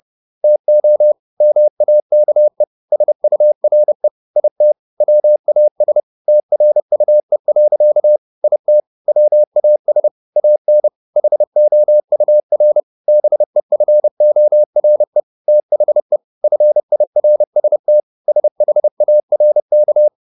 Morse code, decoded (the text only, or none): TO MAKE SURE IT WAS TRUE. IT WAS AN HOUR BEFORE THE FIRST SHARK